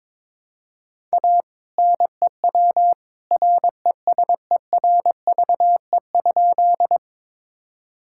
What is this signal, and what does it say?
Morse code, heard as a keyed tone, about 22 wpm, 700 Hz. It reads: A NEW RESERVE?
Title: Sentences from Top 1000 Words - Review Part 6 - 22wpm